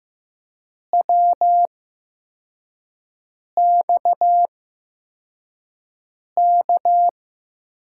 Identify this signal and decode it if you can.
Morse code, heard as a keyed tone, about 15 wpm, spaced as for 4 wpm, 700 Hz. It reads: WXK
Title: USA Call Signs - 8x Character Spacing - 15wpm